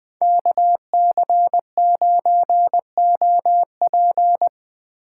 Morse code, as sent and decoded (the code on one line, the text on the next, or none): -.- -.-. ----. --- .--.
KC9OP